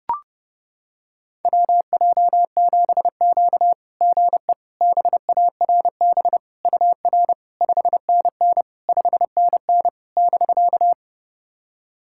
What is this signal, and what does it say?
Morse code, heard as a keyed tone, about 30 wpm, 690 Hz. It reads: WJ7Q GE BARB UR 5NN 5NN <BK>
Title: SOTA QSO Practice 30wpm